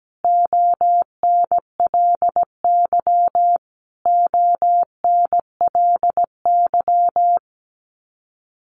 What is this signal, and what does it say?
Morse code, heard as a keyed tone, about 17 wpm, 700 Hz. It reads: ONLY ONLY
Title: Top 500 Words Repeated 17wpm